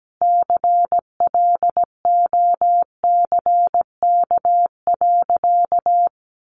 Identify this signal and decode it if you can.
Morse code, heard as a keyed tone, about 17 wpm, 700 Hz. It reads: CLOCK.
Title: Easy Phrases 17wpm